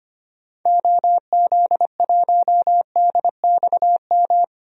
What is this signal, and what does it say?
Morse code, heard as a keyed tone, about 25 wpm, 700 Hz. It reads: OZ1DXM
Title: DXCC-100 Call Signs - 6 Characters - 25wpm